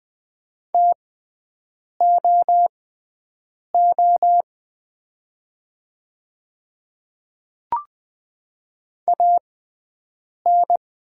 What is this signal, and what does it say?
Morse code, heard as a keyed tone, about 20 wpm, spaced as for 7 wpm, 700 Hz. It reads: TOO AN
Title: 6x Character Spacing - Top 300 Words - 20wpm